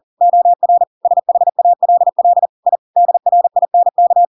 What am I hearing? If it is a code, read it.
Morse code, heard as a keyed tone, about 40 wpm, 700 Hz. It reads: OR SHALL I DRINK
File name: Pinocchio / Pinocchio ch5 40 40 ik1hge